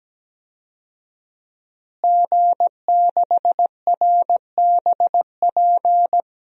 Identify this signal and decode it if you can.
Morse code, heard as a keyed tone, about 17 wpm, 700 Hz. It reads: G6RBP
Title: DXCC-100 Call Signs - 5 Characters - Repeated 2x - 17wpm